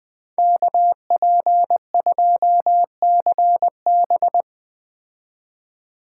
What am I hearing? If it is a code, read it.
Morse code, heard as a keyed tone, about 20 wpm, 700 Hz. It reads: KP2CB
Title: Speed Racing - USA 2x2 Call Signs 17wpm